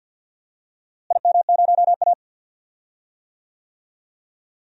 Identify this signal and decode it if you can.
Morse code, heard as a keyed tone, about 52 wpm, 700 Hz. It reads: IM0A
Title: DXCC-100 Call Signs - 4 Characters - Repeated 3x - 50wpm